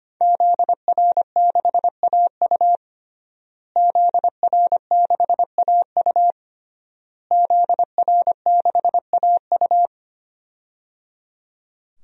Morse code, heard as a keyed tone, about 25 wpm, 700 Hz. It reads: ZR6AU ZR6AU ZR6AU
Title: DXCC-100 Call Signs - 5 Characters - Repeated 3x - 25wpm